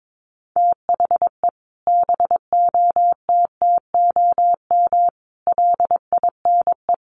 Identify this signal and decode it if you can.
Morse code, heard as a keyed tone, about 22 wpm, 700 Hz. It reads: THE BOTTOM LINE
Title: Sets of 5 Words Encore 22wpm